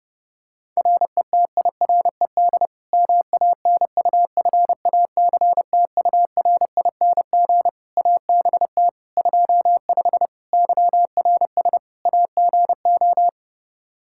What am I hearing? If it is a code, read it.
Morse code, heard as a keyed tone, about 30 wpm, 700 Hz. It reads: RETIRED MANUFACTURING ABT 25 YRS AGO
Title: K1USN SST QSO Practice 30wpm